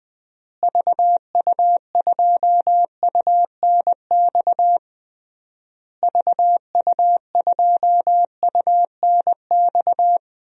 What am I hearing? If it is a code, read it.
Morse code, heard as a keyed tone, about 20 wpm, 700 Hz. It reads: VU2UNX VU2UNX
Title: DXCC-100 Call Signs - 6 Characters - Repeated 3x - 20wpm